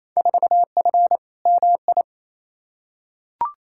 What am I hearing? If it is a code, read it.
Morse code, heard as a keyed tone, about 28 wpm, 700 Hz. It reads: 4F MI
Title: Field Day Exchanges Encore - 28wpm